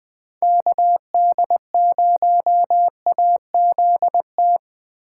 Morse code, heard as a keyed tone, about 20 wpm, 700 Hz. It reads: KD0AZT
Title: Speed Racing - USA 2x3 Call Signs 17wpm